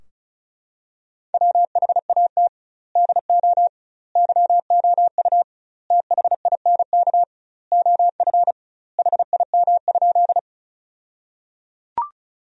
Morse code, heard as a keyed tone, about 36 wpm, 700 Hz. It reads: WHAT DO YOU THINK OF HIM?
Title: 2x Word-Spacing - Sentences from Top 200 Words - 35wpm